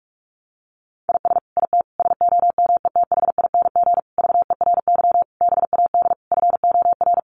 Morse code, heard as a keyed tone, about 44 wpm, 700 Hz. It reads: IS IT SOMETHING VERY BAD FOR
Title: Sentences from Top 700 Words 45wpm